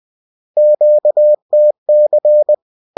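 Morse code, heard as a keyed tone, about 20 wpm, 600 Hz. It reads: QTC
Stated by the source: Bulletin sent in CW.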